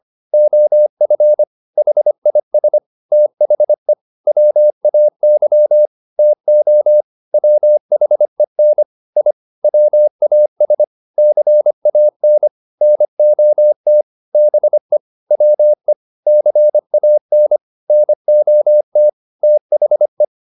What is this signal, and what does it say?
Morse code, heard as a keyed tone, about 25 wpm, 600 Hz. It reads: OF HIS THE WAY TO WHEN I WAS CAN NOT BE WE CAN NOT THE